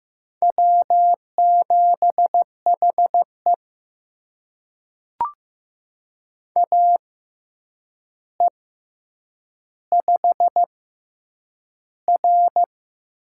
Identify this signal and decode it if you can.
Morse code, heard as a keyed tone, about 15 wpm, spaced as for 8 wpm, 700 Hz. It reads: W7HE AE5R
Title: USA Call Signs - 4 Characters - 6x Character Spacing - 15wpm